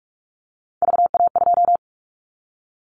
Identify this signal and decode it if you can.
Morse code, heard as a keyed tone, about 46 wpm, 700 Hz. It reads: VA2
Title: Callsign Prefixes for Top 100 DXCC - 45wpm